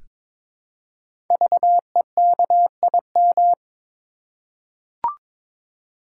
Morse code, heard as a keyed tone, about 22 wpm, 700 Hz. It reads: VEKIM
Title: Code Groups - 5 Letters - 22wpm